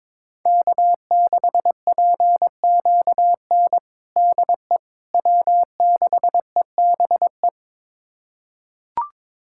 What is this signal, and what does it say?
Morse code, heard as a keyed tone, about 21 wpm, 700 Hz. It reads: K6PQN DE W6EBE
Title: USA Callsign DE Callsign - 5 Characters - 22wpm